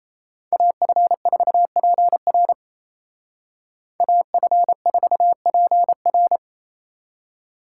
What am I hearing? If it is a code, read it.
Morse code, heard as a keyed tone, about 33 wpm, 700 Hz. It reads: AF4PR AF4PR
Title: Speed Racing - USA Call Signs 28wpm